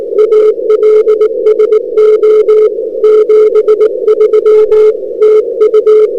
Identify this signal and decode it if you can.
Morse code, heard as a keyed tone, about 22 wpm, 450 Hz. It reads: ALSO 73 TU
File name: n6tr / UA4HAU Nov 22 2008 Morning